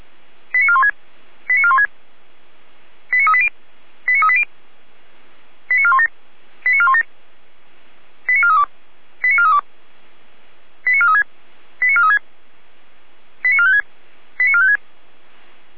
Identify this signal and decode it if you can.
Five-tone selective call (ZVEI), heard as quick runs of five tones, tones of about 70 ms. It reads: 89416 89416 89380 89380 89417 89417 89432 89432 89436 89436 89456 89456